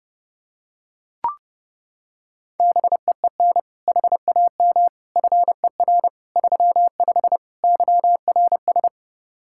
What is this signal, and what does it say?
Morse code, heard as a keyed tone, about 30 wpm, 700 Hz. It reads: BEEN HAM FER 35 YRS
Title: Practice QSO - Exchange 3 - More Info - 30wpm